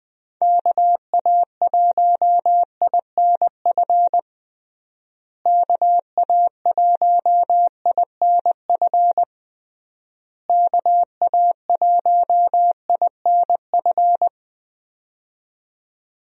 Morse code, heard as a keyed tone, about 20 wpm, 700 Hz. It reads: KA1INF KA1INF KA1INF
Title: USA Call Signs - 6 Characters - Repeated 3x - 20wpm